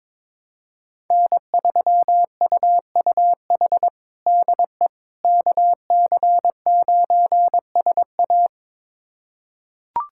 Morse code, heard as a keyed tone, about 22 wpm, 700 Hz. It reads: N3UUH DE KC9SA
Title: USA Callsign DE Callsign - 5 Characters - 22wpm